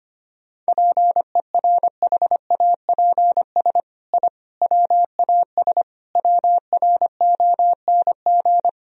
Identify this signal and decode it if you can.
Morse code, heard as a keyed tone, about 25 wpm, 700 Hz. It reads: PERHAPS I WAS WRONG
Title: Sentences from Top 1000 Words - Review Part 1 - 25wpm